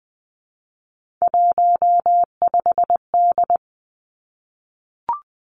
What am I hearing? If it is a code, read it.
Morse code, heard as a keyed tone, about 20 wpm, 700 Hz. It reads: 15D
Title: Three Challenging Letter-Number Combinations 20wpm